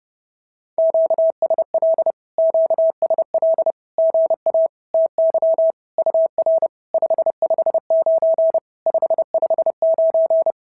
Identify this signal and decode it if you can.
Morse code, heard as a keyed tone, about 30 wpm, 650 Hz. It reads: QSL QSL GA TY UR 559 559
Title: SOTA QSO Practice 30wpm